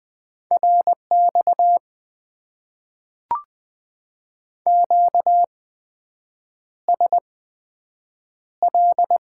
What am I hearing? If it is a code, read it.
Morse code, heard as a keyed tone, about 20 wpm, spaced as for 7 wpm, 700 Hz. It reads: RX QSL
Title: Top 90 QSO Elements - 8x Character Spacing - 20wpm